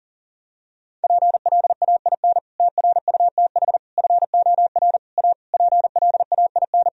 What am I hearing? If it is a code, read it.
Morse code, heard as a keyed tone, about 40 wpm, 700 Hz. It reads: PLAIN TRUTH FOR A PLAIN
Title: Sentences from Top 1000 Words - Review Part 2 - 40wpm